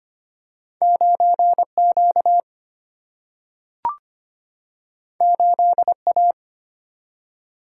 Morse code, heard as a keyed tone, about 25 wpm, 700 Hz. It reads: 9Q 8A
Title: Single-Letter Single-Number 25wpm